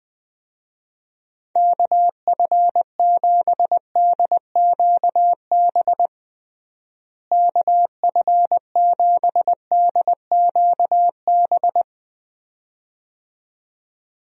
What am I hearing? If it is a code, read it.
Morse code, heard as a keyed tone, about 21 wpm, 700 Hz. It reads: KF7DQB KF7DQB
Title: USA Call Signs - 6 Characters - Repeated 2x - 20wpm